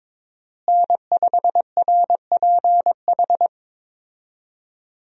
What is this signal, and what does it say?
Morse code, heard as a keyed tone, about 21 wpm, 700 Hz. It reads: N5RPH
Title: USA Call Signs - 5 Characters - Repeated 2x - 22wpm